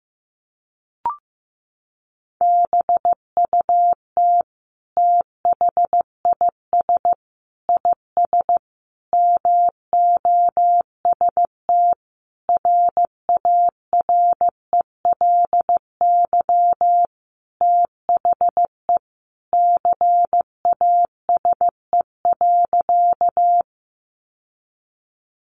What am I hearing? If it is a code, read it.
Morse code, heard as a keyed tone, about 15 wpm, 700 Hz. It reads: BUT THIS IS MOST RARELY THE CASE.
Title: Sentences from Top 1400 Words 15wpm